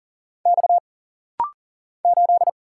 Morse code, heard as a keyed tone, about 40 wpm, 700 Hz. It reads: X 8
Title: Single Letter-Number Alt - Rapid-Fire 40wpm